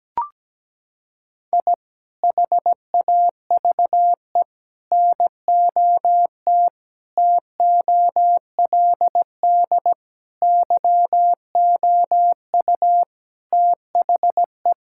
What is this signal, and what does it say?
Morse code, heard as a keyed tone, about 17 wpm, 700 Hz. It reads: I HAVE NOT TOLD YOU THE
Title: Sentences from Top 1000 Words - Review Part 5 - 17wpm